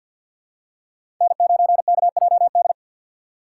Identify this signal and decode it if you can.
Morse code, heard as a keyed tone, about 51 wpm, 700 Hz. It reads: N9KJD
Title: USA Call Signs - 3x Character Spacing - 50wpm